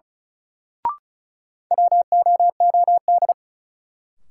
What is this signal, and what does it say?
Morse code, heard as a keyed tone, about 36 wpm, 700 Hz. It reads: WOOD
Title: Sentences from Top 1000 Words - Review Part 2 - New Word Drill - 35wpm